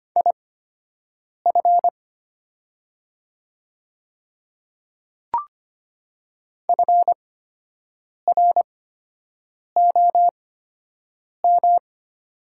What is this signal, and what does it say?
Morse code, heard as a keyed tone, about 25 wpm, spaced as for 7 wpm, 700 Hz. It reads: IF FROM
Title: 8x Character Spacing - Top 100 Words - 25wpm